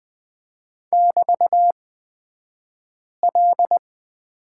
Morse code, heard as a keyed tone, about 20 wpm, spaced as for 5 wpm, 700 Hz. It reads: <BT>L
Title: 223 Lesson 36d - BT - Two Characters 20wpm